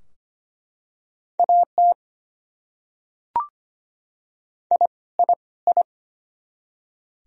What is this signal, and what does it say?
Morse code, heard as a keyed tone, about 25 wpm, 700 Hz. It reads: AT I I I
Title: Top 100 Words Repeated 25wpm